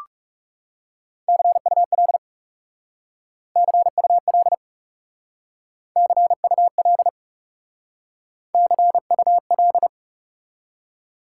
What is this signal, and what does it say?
Morse code, heard as a keyed tone, about 41 wpm, 700 Hz. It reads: CUL CUL CUL CUL